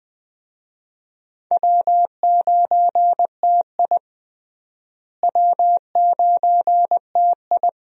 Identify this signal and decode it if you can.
Morse code, heard as a keyed tone, about 21 wpm, 700 Hz. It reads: W9TI W9TI
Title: USA Call Signs - 4 Characters - Repeated 3x - 20wpm